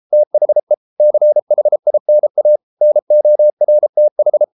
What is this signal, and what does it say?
Morse code, heard as a keyed tone, about 33 wpm, 600 Hz. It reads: THE CHINA NORTH